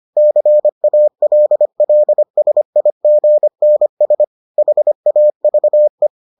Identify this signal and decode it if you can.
Morse code, heard as a keyed tone, about 25 wpm, 600 Hz. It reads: CALLSIGNS HAVE